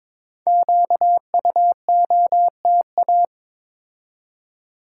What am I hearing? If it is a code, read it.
Morse code, heard as a keyed tone, about 23 wpm, 700 Hz. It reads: QUOTA
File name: 5 Letter Words - First 3 Letters Given - 22wpm